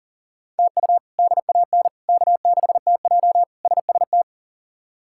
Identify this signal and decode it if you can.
Morse code, heard as a keyed tone, about 37 wpm, 700 Hz. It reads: TU DAN K6TJ SST
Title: K1USN SST QSO Practice - 40wpm